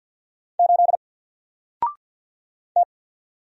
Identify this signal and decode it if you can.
Morse code, heard as a keyed tone, about 50 wpm, 700 Hz. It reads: 8 T
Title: Single Letter-Number Alt 50wpm